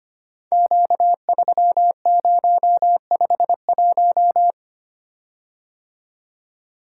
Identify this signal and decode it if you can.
Morse code, heard as a keyed tone, about 25 wpm, 700 Hz. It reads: Q3051
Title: Code Groups - 5 Letter:Numbers - 25wpm